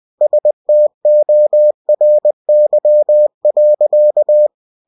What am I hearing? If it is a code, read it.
Morse code, heard as a keyed tone, about 20 wpm, 600 Hz. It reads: STORY.